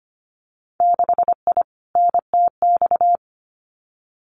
Morse code, heard as a keyed tone, about 25 wpm, 690 Hz. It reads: 6I NTX
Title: Winter Field Day Exchanges Encore 25wpm